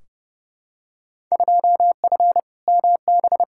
Morse code, heard as a keyed tone, about 30 wpm, 700 Hz. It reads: 2F MB
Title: Field Day Exchanges Encore - 30wpm